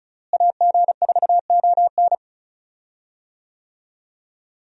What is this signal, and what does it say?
Morse code, heard as a keyed tone, about 34 wpm, 700 Hz. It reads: AG4ON
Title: USA Call Signs - 5 Characters - Repeated 2x - 35wpm